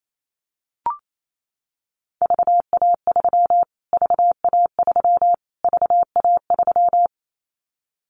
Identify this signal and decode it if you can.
Morse code, heard as a keyed tone, about 28 wpm, 700 Hz. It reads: VA3 VA3 VA3
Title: Callsign Prefixes for Top 25 DXCC - Repeated 3x - 28wpm